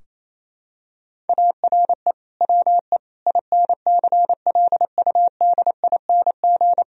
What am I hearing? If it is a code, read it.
Morse code, heard as a keyed tone, about 27 wpm, 700 Hz. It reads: ARE WE INCLUDING
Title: Sentences from Top 1000 Words - Review Part 4 - 28wpm